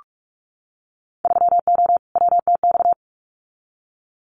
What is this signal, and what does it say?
Morse code, heard as a keyed tone, about 46 wpm, 690 Hz. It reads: 3O WTX
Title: Winter Field Day Exchanges Encore 45wpm